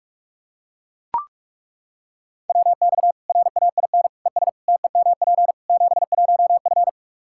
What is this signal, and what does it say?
Morse code, heard as a keyed tone, about 42 wpm, 700 Hz. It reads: WX RAIN ES TEMP 71F
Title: K1USN SST QSO Practice 45wpm